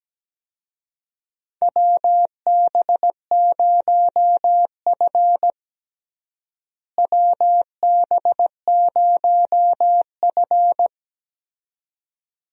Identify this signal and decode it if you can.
Morse code, heard as a keyed tone, about 17 wpm, 700 Hz. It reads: WB0F WB0F
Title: USA Call Signs - 4 Characters - Repeated 2x - 17wpm